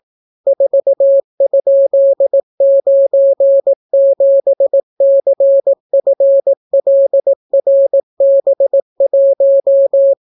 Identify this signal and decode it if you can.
Morse code, heard as a keyed tone, about 18 wpm, 550 Hz. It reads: 4?97CFLRB1